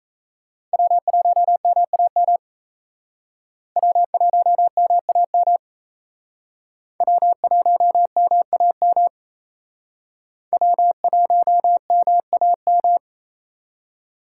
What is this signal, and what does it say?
Morse code, heard as a keyed tone, about 33 wpm, 700 Hz. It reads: W1MAM W1MAM W1MAM W1MAM
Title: Speed Racing - USA 1x3 Call Signs 28wpm